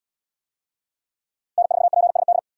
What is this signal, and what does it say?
Morse code, heard as a keyed tone, about 68 wpm, 700 Hz. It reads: N4PSD